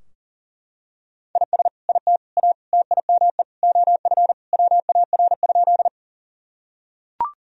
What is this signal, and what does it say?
Morse code, heard as a keyed tone, about 40 wpm, 700 Hz. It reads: IS IT A TIME OF WAR?
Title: Sentences from Top 1000 Words - Review Part 2 - 40wpm